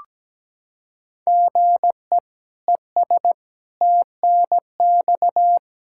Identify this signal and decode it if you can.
Morse code, heard as a keyed tone, about 17 wpm, 700 Hz. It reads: GE ES TNX
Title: Practice QSO - Exchange 1 - Beginning - 17wpm